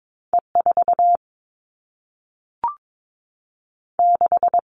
Morse code, heard as a keyed tone, about 22 wpm, 700 Hz. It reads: E 4 6
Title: Two Challenging Letter-Number Combinations 22wpm